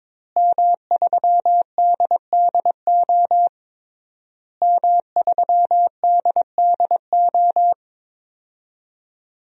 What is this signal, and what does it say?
Morse code, heard as a keyed tone, about 22 wpm, 700 Hz. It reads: M3DDO M3DDO
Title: DXCC-100 Call Signs - 5 Characters - Repeated 3x - 22wpm